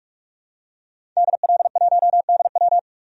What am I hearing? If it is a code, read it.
Morse code, heard as a keyed tone, about 44 wpm, 700 Hz. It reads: DL1DW
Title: DXCC-100 Call Signs - 5 Characters - 45wpm